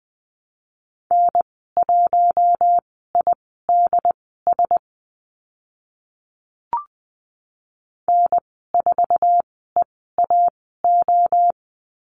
Morse code, heard as a keyed tone, about 20 wpm, spaced as for 14 wpm, 700 Hz. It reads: N1IDS N4EAO
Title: 076 Lesson 13g - Letter L - Call Signs 20wpm